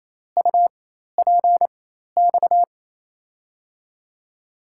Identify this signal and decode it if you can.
Morse code, heard as a keyed tone, about 28 wpm, spaced as for 13 wpm, 700 Hz. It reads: UPX
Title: USA Call Signs - 6 Characters - 4x Character Spacing - 28wpm